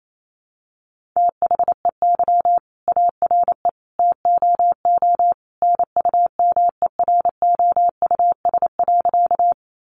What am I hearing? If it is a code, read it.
Morse code, heard as a keyed tone, about 28 wpm, 700 Hz. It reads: THEY ARE TOO NUMEROUS.
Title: Sentences from Top 1400 Words 28wpm